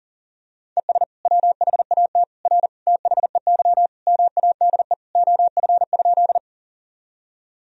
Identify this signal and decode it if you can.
Morse code, heard as a keyed tone, about 40 wpm, 700 Hz. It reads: ES WHAT R THEY MADE OF?
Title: Sentences from Top 200 Words and Abbreviations 40wpm